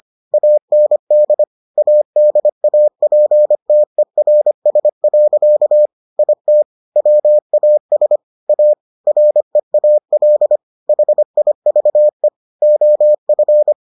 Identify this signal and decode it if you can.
Morse code, heard as a keyed tone, about 25 wpm, 600 Hz. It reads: AND ADAPTERS. IT WAS A REAL HIVE OF